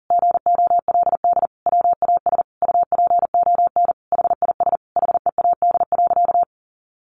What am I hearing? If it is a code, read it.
Morse code, heard as a keyed tone, about 40 wpm, 700 Hz. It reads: GOLD WAS UPON HIS HEAD.
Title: Sentences from Top 1100 Words 40wpm